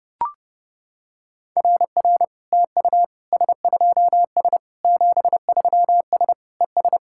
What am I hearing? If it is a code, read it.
Morse code, heard as a keyed tone, about 32 wpm, 700 Hz. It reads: RR TU S2S 73S ES
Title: SOTA QSO Practice 30wpm